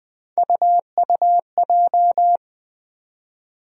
Morse code, heard as a keyed tone, about 20 wpm, 700 Hz. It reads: UUJ
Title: USA Call Sign Postfixes - 3 Letters 20wpm